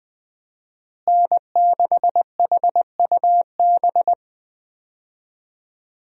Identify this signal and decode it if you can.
Morse code, heard as a keyed tone, about 20 wpm, 700 Hz. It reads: N6HUB
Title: USA Call Signs - 5 Characters - Repeated 3x - 20wpm